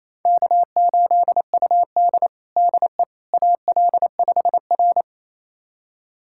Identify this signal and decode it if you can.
Morse code, heard as a keyed tone, about 28 wpm, 700 Hz. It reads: K8UD DE AL5R
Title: USA Callsign DE Callsign - 4 Characters - 28wpm